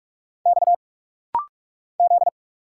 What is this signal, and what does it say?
Morse code, heard as a keyed tone, about 46 wpm, 700 Hz. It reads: X Z
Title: Single Letters - Rapid-Fire 45wpm